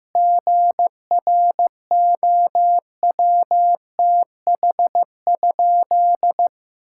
Morse code, heard as a keyed tone, about 15 wpm, 700 Hz. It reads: GROWTH?
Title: Sentences from Top 1000 Words - Review Part 5 - 15wpm